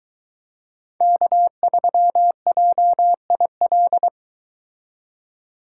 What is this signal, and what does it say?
Morse code, heard as a keyed tone, about 22 wpm, 700 Hz. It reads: K3JIL